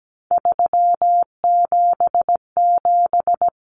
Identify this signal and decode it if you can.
Morse code, heard as a keyed tone, about 17 wpm, 700 Hz. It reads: 377
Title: Common Cut Numbers - 3-Digit 17wpm